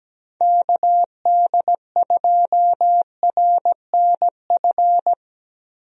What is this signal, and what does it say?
Morse code, heard as a keyed tone, about 17 wpm, 700 Hz. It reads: KD2RNF
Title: USA Call Signs - 6 Characters - 17wpm